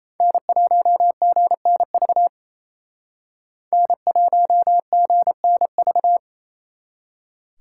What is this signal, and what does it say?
Morse code, heard as a keyed tone, about 36 wpm, 700 Hz. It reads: N1GNV N1GNV